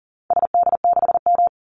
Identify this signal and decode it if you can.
Morse code, heard as a keyed tone, about 38 wpm, 700 Hz. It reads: SD6M
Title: Call Signs - 40wpm